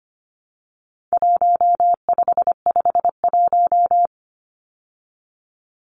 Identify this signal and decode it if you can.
Morse code, heard as a keyed tone, about 25 wpm, 700 Hz. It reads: 1551
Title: Numbers - Contest Style - 4-Digit 25wpm